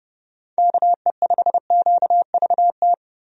Morse code, heard as a keyed tone, about 32 wpm, 700 Hz. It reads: KE5QVT
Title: USA Call Signs - 6 Characters - 30wpm